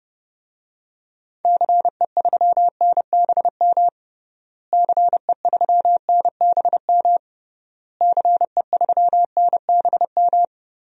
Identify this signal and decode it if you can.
Morse code, heard as a keyed tone, about 32 wpm, 700 Hz. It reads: CE3NBM CE3NBM CE3NBM
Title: DXCC-100 Call Signs - 6 Characters - Repeated 3x - 30wpm